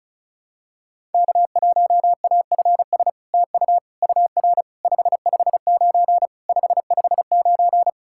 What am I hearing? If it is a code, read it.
Morse code, heard as a keyed tone, about 36 wpm, 700 Hz. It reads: K1AFS TU UR 559 559
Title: SOTA QSO Practice 35wpm